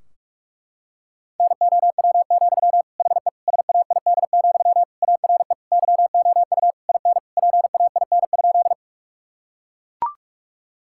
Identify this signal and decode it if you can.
Morse code, heard as a keyed tone, about 41 wpm, 700 Hz. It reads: NOW, HE SAID, ARE YOU IN PAIN?
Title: Sentences from Top 1600 Words 45wpm